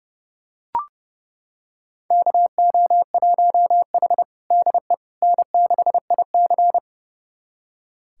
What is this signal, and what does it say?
Morse code, heard as a keyed tone, about 32 wpm, 700 Hz. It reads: KO1H DE N6IC
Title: USA Callsign DE Callsign - 4 Characters - 30wpm